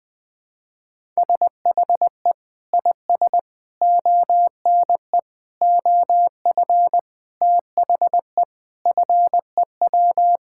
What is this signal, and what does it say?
Morse code, heard as a keyed tone, about 21 wpm, 700 Hz. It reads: SHE IS ONE OF THE FEW
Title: Sentences with 5-Word Phrases - 20wpm